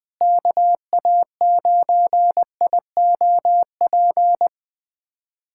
Morse code, heard as a keyed tone, about 21 wpm, 700 Hz. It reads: KA9IOP
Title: USA Callsign DE Callsign 20wpm